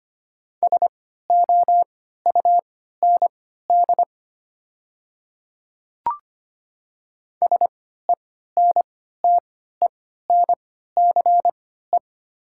Morse code, heard as a keyed tone, about 25 wpm, spaced as for 14 wpm, 690 Hz. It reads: SOUND SENTENCE
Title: 3x Character Spacing - Top 300 Words - 25wpm